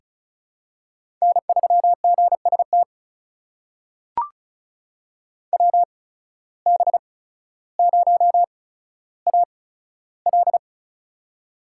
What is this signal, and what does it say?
Morse code, heard as a keyed tone, about 36 wpm, spaced as for 15 wpm, 700 Hz. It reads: N3GST WB0AL
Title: USA Call Signs - 5 Characters - 8x Character Spacing - 35wpm